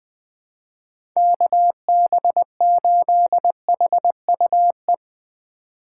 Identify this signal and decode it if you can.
Morse code, heard as a keyed tone, about 23 wpm, 700 Hz. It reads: KB8HUE
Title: Speed Racing - USA 2x3 Call Signs 20wpm